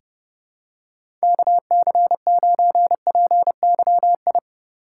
Morse code, heard as a keyed tone, about 29 wpm, 700 Hz. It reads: KC9PYI
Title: USA Call Signs - 6 Characters - 4x Character Spacing - 30wpm